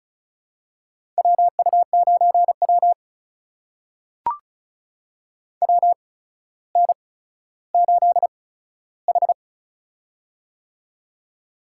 Morse code, heard as a keyed tone, about 36 wpm, spaced as for 15 wpm, 700 Hz. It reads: WU9W WN8H
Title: USA Call Signs - 4 Characters - 8x Character Spacing - 35wpm